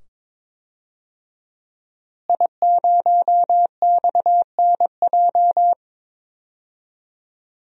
Morse code, heard as a keyed tone, about 22 wpm, 700 Hz. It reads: I0XNJ
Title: DXCC-100 Call Signs - 5 Characters - Repeated 2x - 22wpm